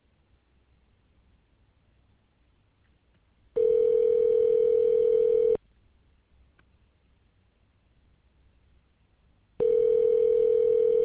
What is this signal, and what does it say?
Ringback tone (North America), 440 + 480 Hz.